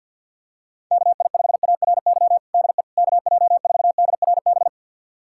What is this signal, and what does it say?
Morse code, heard as a keyed tone, about 52 wpm, 700 Hz. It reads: KI5ARY DE KJ4DRB
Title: USA Callsign DE Callsign 50wpm